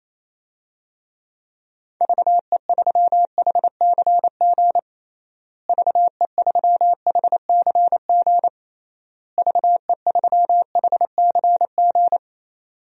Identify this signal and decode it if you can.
Morse code, heard as a keyed tone, about 28 wpm, 700 Hz. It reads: VE3HCG VE3HCG VE3HCG
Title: DXCC-100 Call Signs - 6 Characters - Repeated 3x - 28wpm